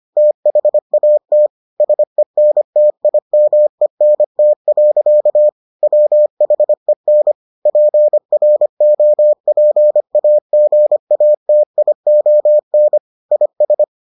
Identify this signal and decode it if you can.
Morse code, heard as a keyed tone, about 25 wpm, 600 Hz. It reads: THAT SENTIMENT. WHEN PROPAGATION IS